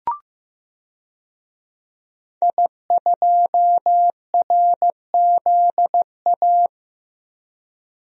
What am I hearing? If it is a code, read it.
Morse code, heard as a keyed tone, about 16 wpm, 700 Hz. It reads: I2RZA
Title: DXCC-100 Call Signs - 5 Characters - Repeated 2x - 15wpm